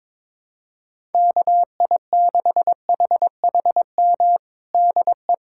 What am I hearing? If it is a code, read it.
Morse code, heard as a keyed tone, about 22 wpm, 700 Hz. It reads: KI6HHM DE
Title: USA Callsign DE Callsign 22wpm